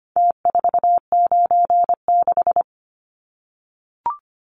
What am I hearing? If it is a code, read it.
Morse code, heard as a keyed tone, about 25 wpm, 700 Hz. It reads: T496
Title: Numbers - Contest Style - 4-Digit 25wpm